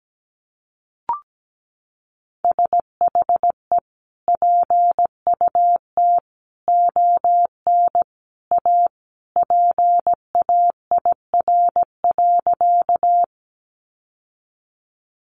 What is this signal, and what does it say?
Morse code, heard as a keyed tone, about 17 wpm, 700 Hz. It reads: SHE PUT ON A PAIR.
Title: Sentences from Top 1200 Words 17wpm